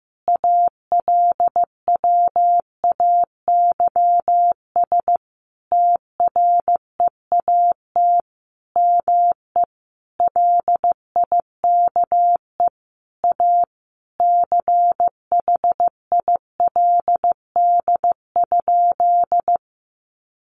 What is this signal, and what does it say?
Morse code, heard as a keyed tone, about 15 wpm, 700 Hz. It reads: ALWAYS TREAT ME LIKE A CHILD?
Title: Sentences from Top 1300 Words 15wpm